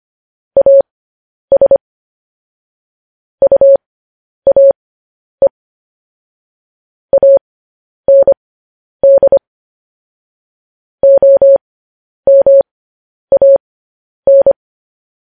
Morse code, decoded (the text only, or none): AS UAE AND OMAN